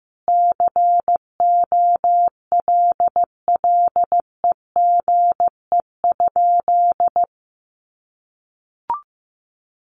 Morse code, heard as a keyed tone, about 15 wpm, 700 Hz. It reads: COLLEGE?
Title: Sentences from Top 1300 Words 15wpm